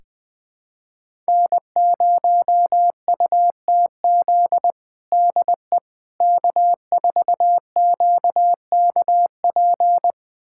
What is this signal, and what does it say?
Morse code, heard as a keyed tone, about 20 wpm, 700 Hz. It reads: N0UTZ DE K4QKP
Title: USA Callsign DE Callsign - 5 Characters - 20wpm